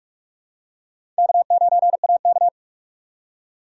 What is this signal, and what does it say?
Morse code, heard as a keyed tone, about 44 wpm, 700 Hz. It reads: K9AK
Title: USA Call Signs - 4 Characters - 3x Character Spacing - 45wpm